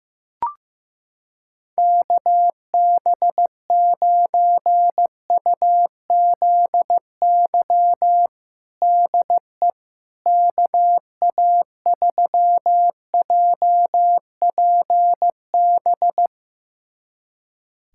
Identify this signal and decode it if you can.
Morse code, heard as a keyed tone, about 15 wpm, 700 Hz. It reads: KB9UZY DE KA3JPB
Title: USA Callsign DE Callsign - 6 Characters - 15wpm